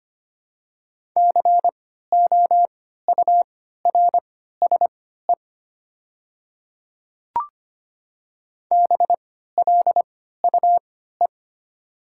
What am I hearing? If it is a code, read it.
Morse code, heard as a keyed tone, about 25 wpm, spaced as for 14 wpm, 700 Hz. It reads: COURSE BLUE
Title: 3x Character Spacing - Top 500 Words - 25wpm